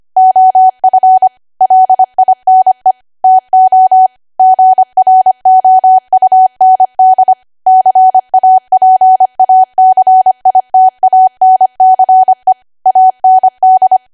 Morse code, heard as a keyed tone, about 25 wpm, 750 Hz. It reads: OF LINE TO GROUND CAPACITANCE AND
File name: cwa / W1AW-Level3 / 25WPM